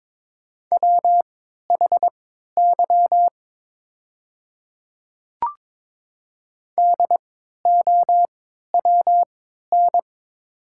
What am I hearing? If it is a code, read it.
Morse code, heard as a keyed tone, about 22 wpm, spaced as for 12 wpm, 700 Hz. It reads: WHY DOWN
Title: 3x Character Spacing - Top 200 Words - 22wpm